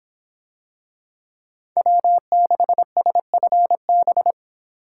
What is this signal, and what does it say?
Morse code, heard as a keyed tone, about 26 wpm, 700 Hz. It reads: W6SFB